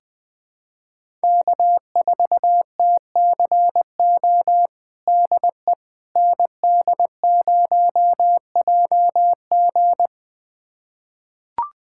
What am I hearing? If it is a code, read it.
Morse code, heard as a keyed tone, about 21 wpm, 700 Hz. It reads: K4TCO DE ND0JG
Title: USA Callsign DE Callsign - 5 Characters - 20wpm